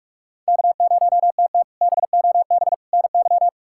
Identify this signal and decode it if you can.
Morse code, heard as a keyed tone, about 42 wpm, 700 Hz. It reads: K0TT BOB NY
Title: K1USN SST QSO Practice - 45wpm